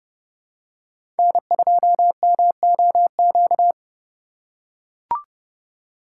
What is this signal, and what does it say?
Morse code, heard as a keyed tone, about 32 wpm, 700 Hz. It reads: N2MOQ
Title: USA Call Signs - 5 Characters - 2x Character Spacing - 30wpm